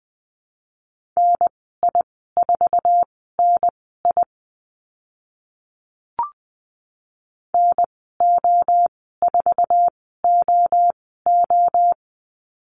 Morse code, heard as a keyed tone, about 20 wpm, spaced as for 15 wpm, 700 Hz. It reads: NI4NI NO4OO
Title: 076 Lesson 13g - Letter L - Call Signs 20wpm